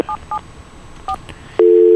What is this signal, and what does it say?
DTMF, heard as telephone touch tones, tones of about 80 ms. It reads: 774